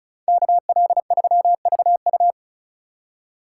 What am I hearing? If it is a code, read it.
Morse code, heard as a keyed tone, about 44 wpm, 700 Hz. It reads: KL3VU